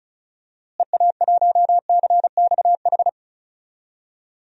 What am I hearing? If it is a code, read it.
Morse code, heard as a keyed tone, about 36 wpm, 700 Hz. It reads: EA1CXH
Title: DXCC-100 Call Signs - 6 Characters - Repeated 3x - 35wpm